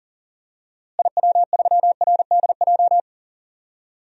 Morse code, heard as a keyed tone, about 40 wpm, 700 Hz. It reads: IW3RDJ
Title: DXCC-100 Call Signs - 6 Characters - Repeated 2x - 40wpm